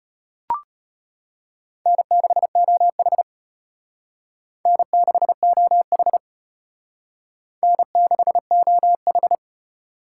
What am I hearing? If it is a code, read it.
Morse code, heard as a keyed tone, about 32 wpm, 700 Hz. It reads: N6OH N6OH N6OH